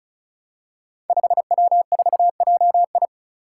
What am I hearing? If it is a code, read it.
Morse code, heard as a keyed tone, about 36 wpm, 700 Hz. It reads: 5W4JI
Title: Five Challenging Letter-Number Combinations 35wpm